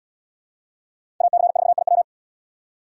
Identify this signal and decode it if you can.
Morse code, heard as a keyed tone, about 77 wpm, 700 Hz. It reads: AC4IW